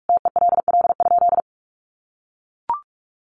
Morse code, heard as a keyed tone, about 44 wpm, 700 Hz. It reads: TELL?
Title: Sentences from Top 300 Words 45wpm